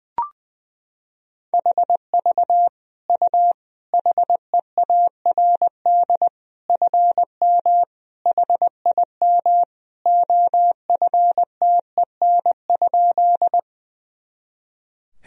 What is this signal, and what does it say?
Morse code, heard as a keyed tone, about 19 wpm, 700 Hz. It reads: HV U HEARD FM HIM OFTEN?
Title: Sentences from Top 400 Words and Abbreviations 20wpm